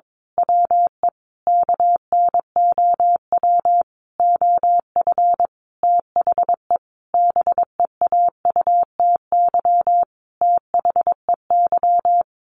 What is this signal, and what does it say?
Morse code, heard as a keyed tone, about 22 wpm, 700 Hz. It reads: WE KNOW OF THE BEAUTY THEY
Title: Sentences from Top 1400 Words 22wpm